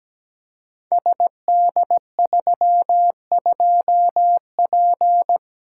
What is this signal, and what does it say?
Morse code, heard as a keyed tone, about 17 wpm, 690 Hz. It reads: SD32P
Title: Code Groups - 5 Letter:Numbers - 17wpm